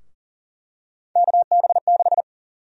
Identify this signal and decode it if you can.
Morse code, heard as a keyed tone, about 40 wpm, 700 Hz. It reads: KB6
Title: USA Call Sign Prefixes - 40wpm